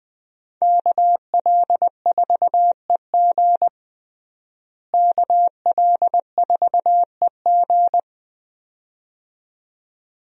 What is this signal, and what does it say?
Morse code, heard as a keyed tone, about 21 wpm, 700 Hz. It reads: KL4EG KL4EG
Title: USA Call Signs - 5 Characters - Repeated 2x - 20wpm